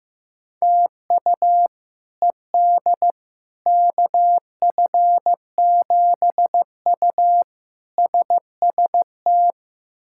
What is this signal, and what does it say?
Morse code, heard as a keyed tone, about 15 wpm, 700 Hz. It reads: TU ED KF7U SST
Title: K1USN SST QSO Practice - 15wpm